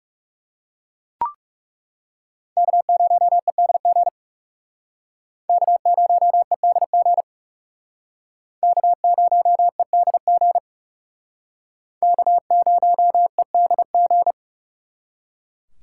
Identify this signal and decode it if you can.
Morse code, heard as a keyed tone, about 36 wpm, 700 Hz. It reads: K0EDG K0EDG K0EDG K0EDG
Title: Speed Racing - USA 1x3 Call Signs 30wpm